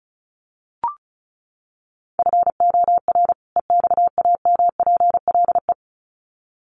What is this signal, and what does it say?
Morse code, heard as a keyed tone, about 34 wpm, 700 Hz. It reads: FOR EXAMPLE
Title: Sets of 2 Words 35wpm